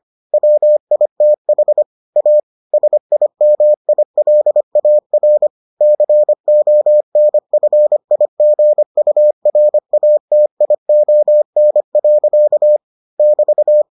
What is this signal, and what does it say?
Morse code, heard as a keyed tone, about 25 wpm, 600 Hz. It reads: WITH A SIMILAR CONFIGURATION. <BT>